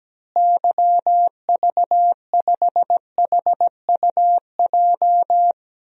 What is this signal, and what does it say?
Morse code, heard as a keyed tone, about 17 wpm, 700 Hz. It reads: YV5HUJ
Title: DXCC-100 Call Signs - 6 Characters - Repeated 2x - 17wpm